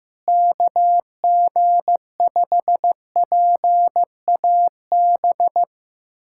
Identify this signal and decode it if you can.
Morse code, heard as a keyed tone, about 16 wpm, 700 Hz. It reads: KG5PAB
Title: USA Callsign DE Callsign - 6 Characters - 15wpm